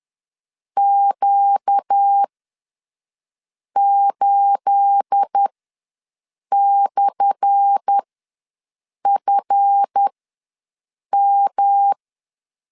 Morse code, heard as a keyed tone, about 10 wpm, spaced as for 5 wpm, 800 Hz. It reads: Q8/FM
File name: LEZIONI / 10 SIGNS / 10A SLASH ws 10wpm 4121